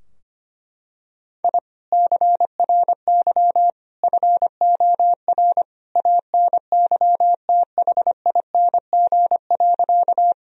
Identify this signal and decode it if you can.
Morse code, heard as a keyed tone, about 25 wpm, 700 Hz. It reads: I CRY FOR ANYTHING.
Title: Sentences from Top 1000 Words - Review Part 2 - 25wpm